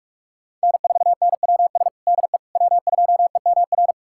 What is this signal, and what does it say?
Morse code, heard as a keyed tone, about 46 wpm, 700 Hz. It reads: N4NWS DE W2EMR
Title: USA Callsign DE Callsign - 5 Characters - 45wpm